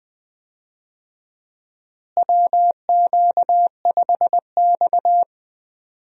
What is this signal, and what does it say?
Morse code, heard as a keyed tone, about 20 wpm, 700 Hz. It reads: WQ5X
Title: USA Call Signs - 4 Characters - Repeated 2x - 20wpm